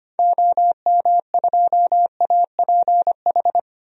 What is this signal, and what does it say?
Morse code, heard as a keyed tone, about 25 wpm, 700 Hz. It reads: OM2APH
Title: DXCC-100 Call Signs - 6 Characters - 25wpm